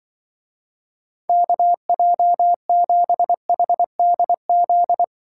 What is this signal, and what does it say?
Morse code, heard as a keyed tone, about 24 wpm, 700 Hz. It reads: KJ7HDZ
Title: Speed Racing - USA Call Signs 20wpm